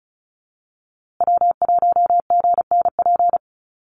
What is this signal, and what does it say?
Morse code, heard as a keyed tone, about 34 wpm, 700 Hz. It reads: W1GNP
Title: USA Call Signs Repeated 3x 35wpm